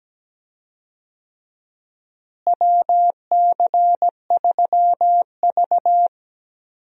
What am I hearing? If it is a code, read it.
Morse code, heard as a keyed tone, about 17 wpm, 700 Hz. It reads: WC3V
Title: USA Call Signs - 4 Characters - Repeated 3x - 17wpm